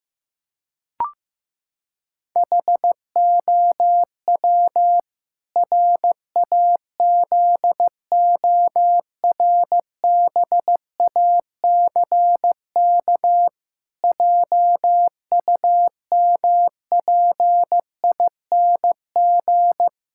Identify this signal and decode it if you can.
Morse code, heard as a keyed tone, about 15 wpm, 700 Hz. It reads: HOW RAZORBACK JUMPING
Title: Pangram Challenge v2 - 15wpm